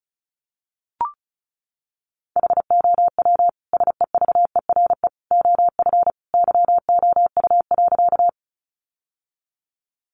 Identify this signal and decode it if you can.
Morse code, heard as a keyed tone, about 34 wpm, 700 Hz. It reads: HOW SEVERE OF YOU.
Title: Sentences from Top 1400 Words 35wpm